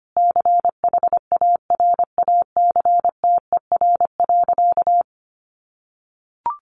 Morse code, heard as a keyed tone, about 25 wpm, 700 Hz. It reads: CHARACTER.
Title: Sentences from Top 1500 Words 25wpm